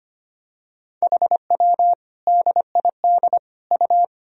Morse code, heard as a keyed tone, about 25 wpm, 700 Hz. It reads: HW DID U
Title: Sentences from Top 100 Words and Abbreviations 25wpm